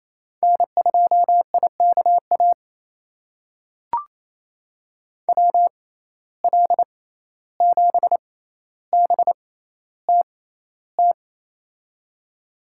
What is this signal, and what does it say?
Morse code, heard as a keyed tone, about 28 wpm, spaced as for 14 wpm, 700 Hz. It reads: N2IKA WL7BTT
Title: USA Call Signs - 6x Character Spacing - 28wpm